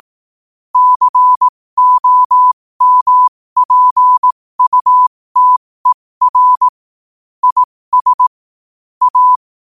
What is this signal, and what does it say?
Morse code, heard as a keyed tone, about 18 wpm, spaced as for 15 wpm, 1000 Hz. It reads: COMPUTER IS A